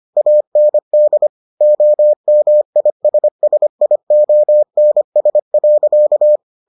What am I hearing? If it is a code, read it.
Morse code, heard as a keyed tone, about 25 wpm, 600 Hz. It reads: AND OMISSIONS.